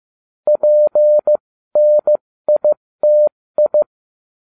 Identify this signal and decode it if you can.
Morse code, heard as a keyed tone, about 15 wpm, spaced as for 13 wpm, 600 Hz. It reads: PNITI
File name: cw / ubacw13 2